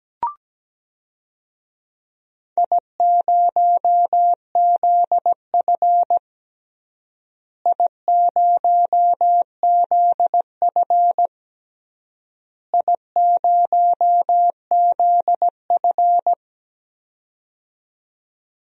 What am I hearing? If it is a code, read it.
Morse code, heard as a keyed tone, about 17 wpm, 700 Hz. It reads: I0ZF I0ZF I0ZF
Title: DXCC-100 Call Signs - 4 Characters - Repeated 3x - 17wpm